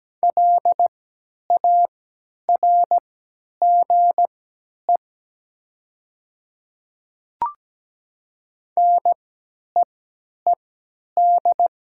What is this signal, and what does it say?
Morse code, heard as a keyed tone, about 17 wpm, spaced as for 10 wpm, 700 Hz. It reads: LARGE NEED
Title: 3x Character Spacing - Top 200 Words - 17wpm